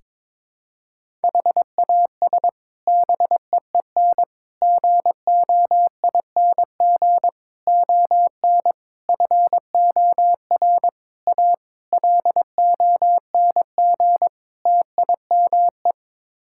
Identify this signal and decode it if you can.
Morse code, heard as a keyed tone, about 22 wpm, 700 Hz. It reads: HAS BEEN GOING ON FOR A LONG TIME
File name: practice / 22wpm / Sets of 8 Words - 22wpm